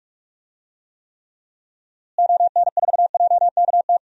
Morse code, heard as a keyed tone, about 42 wpm, 700 Hz. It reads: ON4JKT
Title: DXCC-100 Call Signs - 6 Characters - Repeated 2x - 45wpm